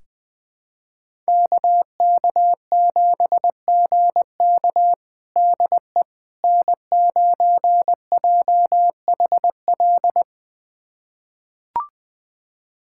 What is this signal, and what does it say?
Morse code, heard as a keyed tone, about 21 wpm, 700 Hz. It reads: KK7GK DE N9JHL
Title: USA Callsign DE Callsign - 5 Characters - 20wpm